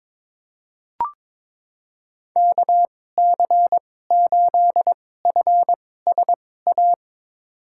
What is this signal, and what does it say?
Morse code, heard as a keyed tone, about 22 wpm, spaced as for 16 wpm, 700 Hz. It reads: KC8FSA
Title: USA Call Signs - 6 Characters - 2x Character Spacing - 22wpm